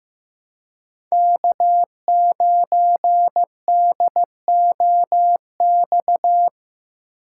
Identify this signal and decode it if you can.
Morse code, heard as a keyed tone, about 15 wpm, 700 Hz. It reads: K9DOX
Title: USA Call Signs - 5 Characters - 2x Character Spacing - 15wpm